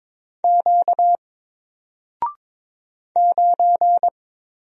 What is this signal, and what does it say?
Morse code, heard as a keyed tone, about 22 wpm, 700 Hz. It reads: Q 9
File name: Single-Letter-Number-22wpm